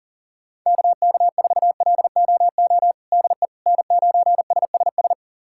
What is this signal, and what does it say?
Morse code, heard as a keyed tone, about 37 wpm, 700 Hz. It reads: KK4LOO DE N9SSS
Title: USA Callsign DE Callsign 40wpm